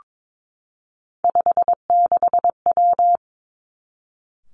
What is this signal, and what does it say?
Morse code, heard as a keyed tone, about 22 wpm, 700 Hz. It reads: 56W
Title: Three Challenging Letter-Number Combinations 22wpm